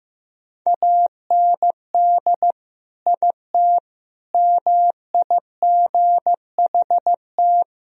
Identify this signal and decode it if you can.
Morse code, heard as a keyed tone, about 16 wpm, 700 Hz. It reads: AND IT MIGHT
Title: Sentences from Top 1000 Words - Review Part 1 - 15wpm